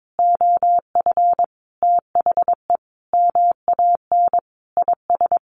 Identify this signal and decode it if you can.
Morse code, heard as a keyed tone, about 22 wpm, 700 Hz. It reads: OF THE MAN IS
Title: Sentences from Top 1400 Words 22wpm